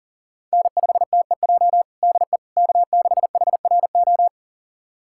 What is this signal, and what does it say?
Morse code, heard as a keyed tone, about 40 wpm, 700 Hz. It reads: N5TEJ DE K6HRO
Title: USA Callsign DE Callsign 40wpm